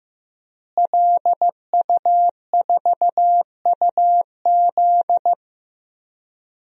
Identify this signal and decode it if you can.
Morse code, heard as a keyed tone, about 15 wpm, 700 Hz. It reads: LU4UZ
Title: DXCC-100 Call Signs - 5 Characters - Repeated 2x - 15wpm